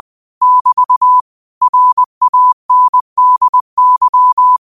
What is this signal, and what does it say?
Morse code, heard as a keyed tone, about 20 wpm, 1000 Hz. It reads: <BT> RANDY
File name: qotdfiles / 20wpm